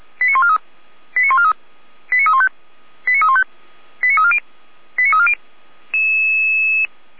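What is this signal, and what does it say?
Five-tone selective call (ZVEI), heard as quick runs of five tones, tones of about 70 ms. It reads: 89143 89143 89315 89315 89340 89340
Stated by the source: ZVEI five-tone call 89305 received by SDR, recorded by feuerwehr-orlamuende.org fire brigade.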